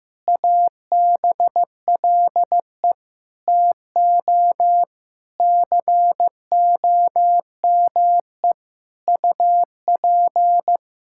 Morse code, heard as a keyed tone, about 15 wpm, 700 Hz. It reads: ABLE TO COME UP
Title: Sentences from Top 1000 Words - Review Part 5 - 15wpm